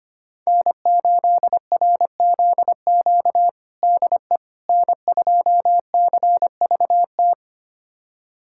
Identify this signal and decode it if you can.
Morse code, heard as a keyed tone, about 25 wpm, 700 Hz. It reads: N8RZQ DE N2CVT
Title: USA Callsign DE Callsign - 5 Characters - 25wpm